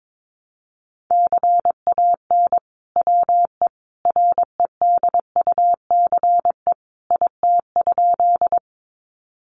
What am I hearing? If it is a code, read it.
Morse code, heard as a keyed tone, about 22 wpm, 700 Hz. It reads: CAN WE REDUCE IT?
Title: Sentences from Top 1100 Words 22wpm